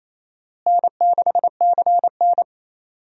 Morse code, heard as a keyed tone, about 28 wpm, 700 Hz. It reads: N6CN
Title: USA Call Signs - 4 Characters - Repeated 3x - 28wpm